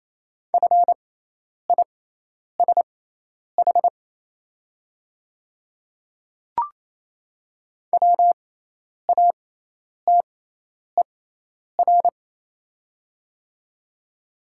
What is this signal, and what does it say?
Morse code, heard as a keyed tone, about 28 wpm, spaced as for 10 wpm, 700 Hz. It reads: FISH WATER
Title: 6x Character Spacing - Top 300 Words - 28wpm